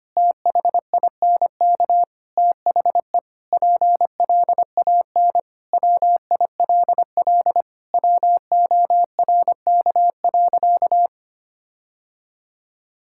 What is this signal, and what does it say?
Morse code, heard as a keyed tone, about 25 wpm, 700 Hz. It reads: THINK THE PLAN WILL WORK.
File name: practice / 25wpm / Sentences with 4-Word Phrases - 25wpm